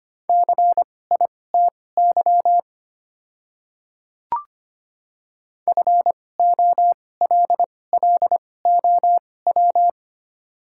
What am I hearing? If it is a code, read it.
Morse code, heard as a keyed tone, about 25 wpm, spaced as for 18 wpm, 700 Hz. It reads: CITY FOLLOW
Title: 2x Character Spacing - Top 300 Words - 25wpm